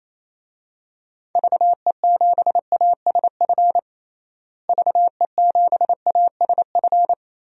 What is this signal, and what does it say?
Morse code, heard as a keyed tone, about 28 wpm, 700 Hz. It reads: VE7ASF VE7ASF
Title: DXCC-100 Call Signs - 6 Characters - Repeated 3x - 28wpm